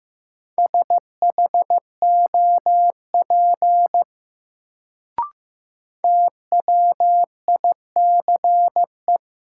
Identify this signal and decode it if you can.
Morse code, heard as a keyed tone, about 15 wpm, 700 Hz. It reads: SHOP TWICE
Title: Sentences from Top 1600 Words - New Word Drill - 15wpm